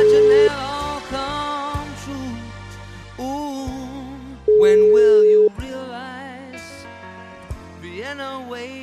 Ringback tone (Europe, 425 Hz).